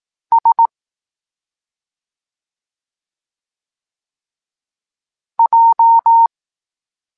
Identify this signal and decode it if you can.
Morse code, heard as a keyed tone, about 18 wpm, 920 Hz. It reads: S J
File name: Tekst-26 1